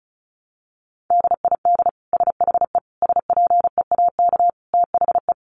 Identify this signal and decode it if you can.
Morse code, heard as a keyed tone, about 34 wpm, 700 Hz. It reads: DID SHE SPEAK THE
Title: Sentences from Top 1100 Words 35wpm